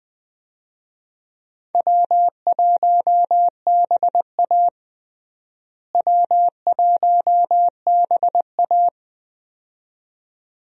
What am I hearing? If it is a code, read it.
Morse code, heard as a keyed tone, about 20 wpm, 700 Hz. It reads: W1BA W1BA
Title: USA Call Signs - 4 Characters - Repeated 2x - 20wpm